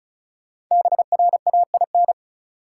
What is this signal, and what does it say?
Morse code, heard as a keyed tone, about 34 wpm, 700 Hz. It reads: BRAIN
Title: Sentences from Top 1100 Words - New Word Drill Alt - 35wpm